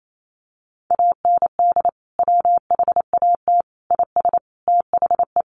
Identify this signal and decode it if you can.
Morse code, heard as a keyed tone, about 28 wpm, 700 Hz. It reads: AND WHAT IS THE
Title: Sentences from Top 1500 Words 28wpm